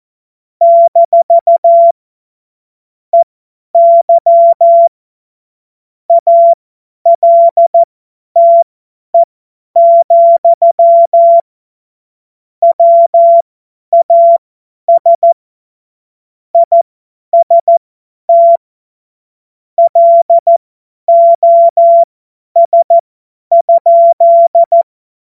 Morse code, heard as a keyed tone, about 14 wpm, spaced as for 10 wpm, 680 Hz. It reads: - EY ALTE, WAS IST LOS?
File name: Morsekurs / Witze / Witz 19. Oktober 2021, 50 F70